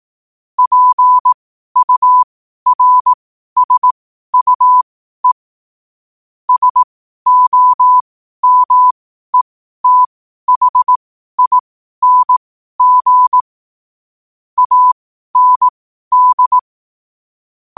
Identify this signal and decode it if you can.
Morse code, heard as a keyed tone, about 18 wpm, spaced as for 12 wpm, 1000 Hz. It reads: PURSUE SOMETHING AND
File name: qotdfiles / 12wpm